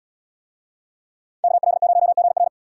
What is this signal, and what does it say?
Morse code, heard as a keyed tone, about 74 wpm, 700 Hz. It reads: KL0GR